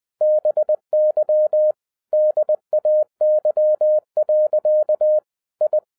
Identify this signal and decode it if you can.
Morse code, heard as a keyed tone, about 20 wpm, 610 Hz. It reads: BY DAY. I
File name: 2012-01-mp3 / n05-CAME BACK TO HAM RADIO AFTER 30 YEARS OF INTERVAL-20wpm